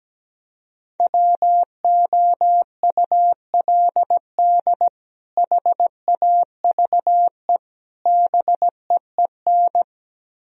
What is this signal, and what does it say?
Morse code, heard as a keyed tone, about 17 wpm, 700 Hz. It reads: WOULD HAVE BEEN
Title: Speed-Racing - Sets of 3 Words 17wpm